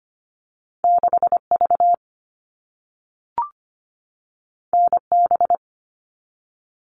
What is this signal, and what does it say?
Morse code, heard as a keyed tone, about 25 wpm, 700 Hz. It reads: 6V NB
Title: All Cut Numbers - 2-Digit 25wpm